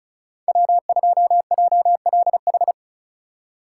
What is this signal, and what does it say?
Morse code, heard as a keyed tone, about 36 wpm, 700 Hz. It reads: W2JLH
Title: USA Call Signs - 5 Characters - 3x Character Spacing - 35wpm